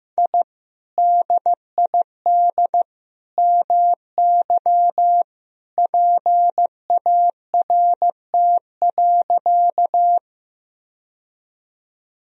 Morse code, heard as a keyed tone, about 15 wpm, 700 Hz. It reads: I DID MY PART.
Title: Sentences from Top 1000 Words - Review Part 1 - 15wpm